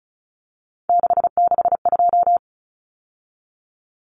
Morse code, heard as a keyed tone, about 34 wpm, 700 Hz. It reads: 662
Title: Numbers - 3-Digit 35wpm